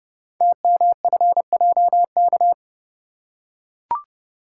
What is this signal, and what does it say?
Morse code, heard as a keyed tone, about 30 wpm, 700 Hz. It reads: TMFJK
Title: Code Groups - 5 Letter:Numbers - 30wpm